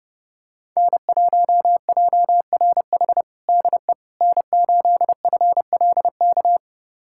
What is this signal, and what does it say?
Morse code, heard as a keyed tone, about 30 wpm, 700 Hz. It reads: N1JRH DE N8FLK
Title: USA Callsign DE Callsign - 5 Characters - 30wpm